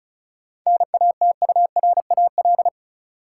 Morse code, heard as a keyed tone, about 36 wpm, 700 Hz. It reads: NATURAL
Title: Sentences from Top 700 Words - New Word Drill - 35wpm